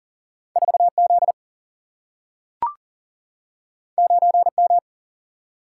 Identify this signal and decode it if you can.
Morse code, heard as a keyed tone, about 40 wpm, 700 Hz. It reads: 4 Z 9 M
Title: Single-Letter Single-Number 40wpm